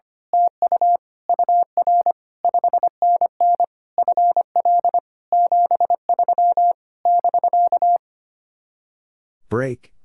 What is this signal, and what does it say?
Morse code, heard as a keyed tone, about 24 wpm, 700 Hz. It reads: TU UR 5NN FL 73 <BK>
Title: Speed-Racing Common QSO Phrases Encore - 25wpm